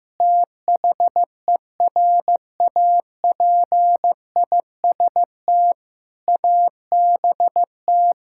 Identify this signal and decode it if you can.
Morse code, heard as a keyed tone, about 15 wpm, 700 Hz. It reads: THERAPIST ABT
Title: Practice QSO - Exchange 3 - More Info - 15wpm